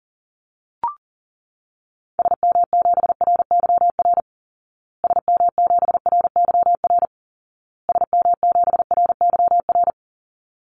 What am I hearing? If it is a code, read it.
Morse code, heard as a keyed tone, about 40 wpm, 700 Hz. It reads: SM7RYR SM7RYR SM7RYR
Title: Call Signs Repeated 2x - 40wpm